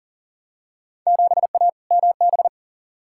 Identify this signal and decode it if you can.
Morse code, heard as a keyed tone, about 37 wpm, 700 Hz. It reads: 7A MB
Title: Field Day Exchanges Encore - 40wpm